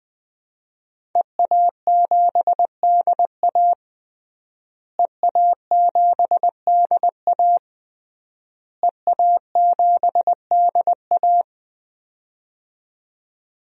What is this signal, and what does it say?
Morse code, heard as a keyed tone, about 20 wpm, 700 Hz. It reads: EA7DA EA7DA EA7DA
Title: DXCC-100 Call Signs - 5 Characters - Repeated 3x - 20wpm